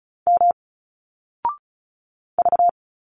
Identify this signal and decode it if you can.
Morse code, heard as a keyed tone, about 34 wpm, 700 Hz. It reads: M V
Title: Single Letters 35wpm